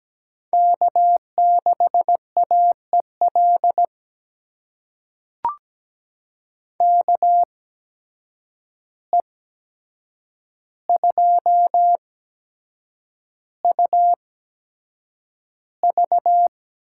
Morse code, heard as a keyed tone, about 17 wpm, spaced as for 7 wpm, 700 Hz. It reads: K6AEL KE2UV
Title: USA Call Signs - 5 Characters - 8x Character Spacing - 17wpm